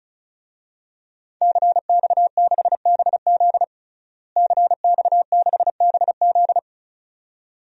Morse code, heard as a keyed tone, about 36 wpm, 700 Hz. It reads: CX6BZ CX6BZ
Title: DXCC-100 Call Signs - 5 Characters - Repeated 2x - 35wpm